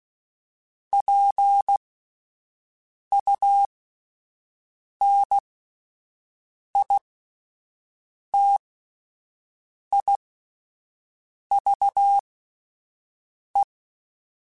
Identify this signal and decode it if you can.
Morse code, heard as a keyed tone, about 16 wpm, spaced as for 5 wpm, 800 Hz. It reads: PUNITIVE